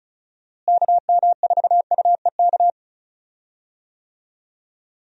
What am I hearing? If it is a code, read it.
Morse code, heard as a keyed tone, about 36 wpm, 700 Hz. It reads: KM4UEK
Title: USA Call Signs - 6 Characters - Repeated 3x - 35wpm